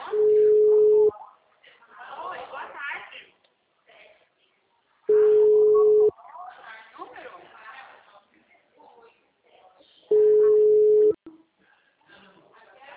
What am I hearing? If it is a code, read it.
Ringback tone (Europe, 425 Hz), with speech.
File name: out- -2001- - - .26729